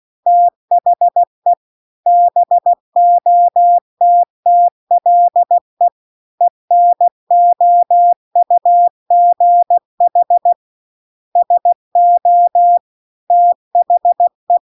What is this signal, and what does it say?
Morse code, heard as a keyed tone, about 16 wpm, 700 Hz. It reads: THE BOTTLE ENOUGH SO THE
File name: 16WPM